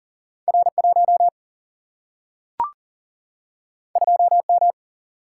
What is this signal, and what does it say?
Morse code, heard as a keyed tone, about 39 wpm, 700 Hz. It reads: R 1 2 M
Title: Single-Letter Single-Number 40wpm